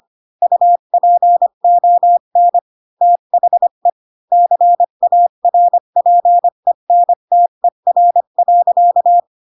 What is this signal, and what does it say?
Morse code, heard as a keyed tone, about 25 wpm, 700 Hz. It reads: UPON THE CARPENTER.